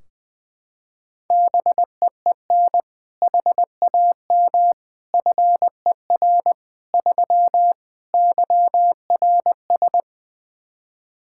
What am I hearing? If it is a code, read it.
Morse code, heard as a keyed tone, about 21 wpm, 700 Hz. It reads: BEEN HAM FER 3 YRS
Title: K1USN SST QSO Practice 20wpm